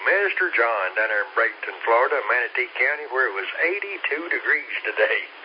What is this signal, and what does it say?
Emergency Alert System (SAME) tones, heard as harsh buzzing bursts, with speech. It is heard from 0.9 to 4.7 s.